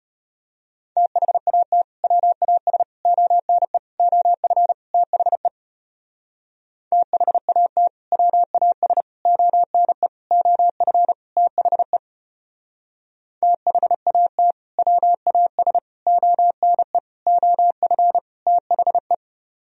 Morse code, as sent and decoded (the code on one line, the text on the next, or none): - .... .- - / .-- .- ... / --- -. / --- ..-. / - .... / - .... .- - / .-- .- ... / --- -. / --- ..-. / - .... / - .... .- - / .-- .- ... / --- -. / --- ..-. / - .... .
THAT WAS ONE OF THE THAT WAS ONE OF THE THAT WAS ONE OF THE